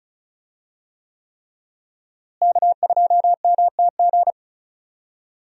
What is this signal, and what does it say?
Morse code, heard as a keyed tone, about 34 wpm, 700 Hz. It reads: K2MTG
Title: USA Call Signs - 5 Characters - Repeated 3x - 35wpm